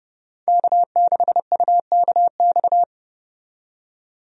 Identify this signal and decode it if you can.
Morse code, heard as a keyed tone, about 30 wpm, 700 Hz. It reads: K6UKX
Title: USA Call Signs - 5 Characters - 30wpm